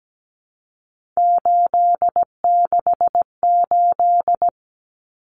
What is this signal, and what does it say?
Morse code, heard as a keyed tone, about 17 wpm, 700 Hz. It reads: 868
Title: Numbers - 3-Digit 17wpm